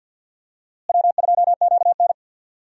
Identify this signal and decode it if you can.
Morse code, heard as a keyed tone, about 46 wpm, 700 Hz. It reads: W2QN
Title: USA Call Signs - 4 Characters - Repeated 3x - 50wpm